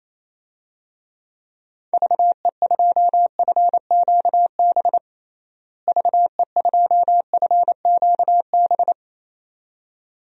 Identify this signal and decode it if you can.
Morse code, heard as a keyed tone, about 28 wpm, 700 Hz. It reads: VE2FQB VE2FQB
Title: DXCC-100 Call Signs - 6 Characters - Repeated 2x - 28wpm